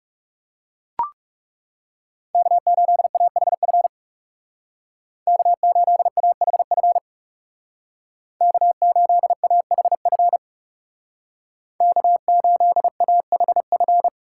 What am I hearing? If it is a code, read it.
Morse code, heard as a keyed tone, about 37 wpm, 700 Hz. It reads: K8AHF K8AHF K8AHF K8AHF
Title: Speed Racing - USA Call Signs 30wpm